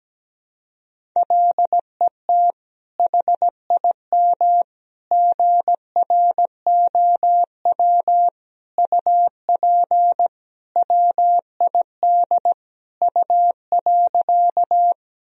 Morse code, heard as a keyed tone, about 17 wpm, 700 Hz. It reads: LET HIM GROW UP WID U.
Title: Sentences from Top 300 Words and Abbreviations 17wpm